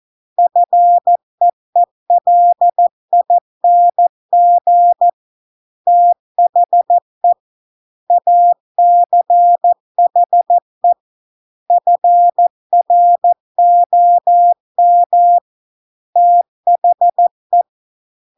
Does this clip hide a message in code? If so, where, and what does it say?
Yes: Morse code from 0.4 to 17.6 s, heard as a keyed tone, about 14 wpm, 700 Hz. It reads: FEELING THE ACHE FROM THE